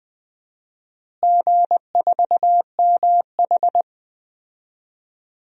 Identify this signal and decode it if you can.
Morse code, heard as a keyed tone, about 20 wpm, 700 Hz. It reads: G4MH
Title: DXCC-100 Call Signs - 4 Characters - Repeated 3x - 20wpm